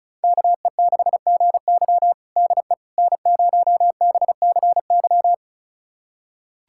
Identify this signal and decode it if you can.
Morse code, heard as a keyed tone, about 36 wpm, 700 Hz. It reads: KE6GY DE N0BCY
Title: USA Callsign DE Callsign 35wpm